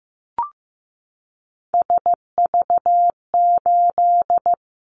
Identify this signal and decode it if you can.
Morse code, heard as a keyed tone, about 15 wpm, 700 Hz. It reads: SV8
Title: Callsign Prefixes for Top 50 DXCC - 15wpm